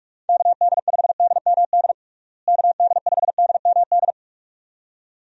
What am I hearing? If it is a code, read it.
Morse code, heard as a keyed tone, about 41 wpm, 700 Hz. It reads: KD5DMD KD5DMD
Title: USA Call Signs - 6 Characters - Repeated 2x - 45wpm